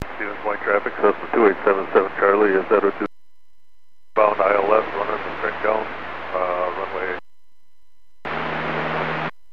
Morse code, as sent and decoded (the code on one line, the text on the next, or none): none